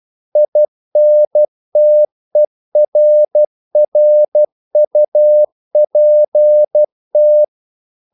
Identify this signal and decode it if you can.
Morse code, heard as a keyed tone, about 12 wpm, 600 Hz. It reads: INTERRUPT
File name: CW / Practice / EnglishQuotations0112